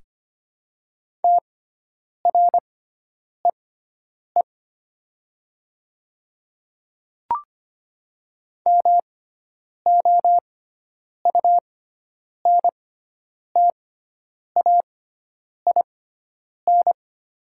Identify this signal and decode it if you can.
Morse code, heard as a keyed tone, about 25 wpm, spaced as for 9 wpm, 700 Hz. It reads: TREE MOUNTAIN
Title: 6x Character Spacing - Top 300 Words - 25wpm